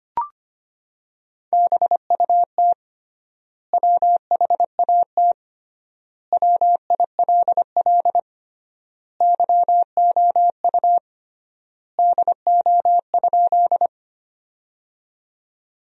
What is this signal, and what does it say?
Morse code, heard as a keyed tone, about 25 wpm, 700 Hz. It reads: BUT WHAT WILL YOU DO?
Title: 3x Word-Spacing - Sentences from Top 100 Words - 25wpm